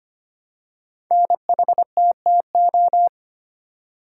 Morse code, heard as a keyed tone, about 25 wpm, 700 Hz. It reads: NHTTO
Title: Code Groups - 5 Letters - 25wpm